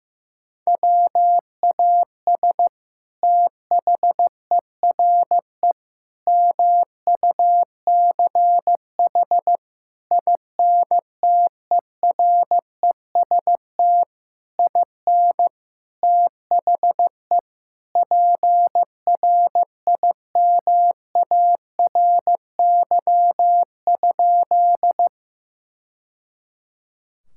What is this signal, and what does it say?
Morse code, heard as a keyed tone, about 15 wpm, 700 Hz. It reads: WAS THERE MUCH INTEREST IN THE PRIMARY?
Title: Sentences from Top 1000 Words - Review Part 5 - 15wpm